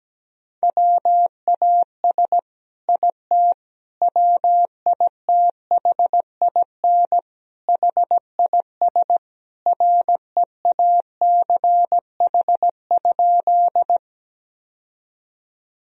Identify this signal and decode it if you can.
Morse code, heard as a keyed tone, about 17 wpm, 700 Hz. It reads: WAS IT WITHIN HIS REACH?
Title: Sentences from Top 1000 Words - Review Part 4 - 17wpm